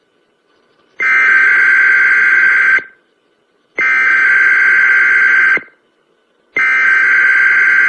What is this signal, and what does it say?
Emergency Alert System (SAME) tones (header bursts), heard as harsh buzzing bursts. It is heard from 1.0 to 7.3 s.